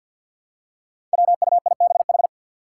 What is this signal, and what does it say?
Morse code, heard as a keyed tone, about 51 wpm, 700 Hz. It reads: WUIBH
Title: Five Challenging Letter-Number Combinations 50wpm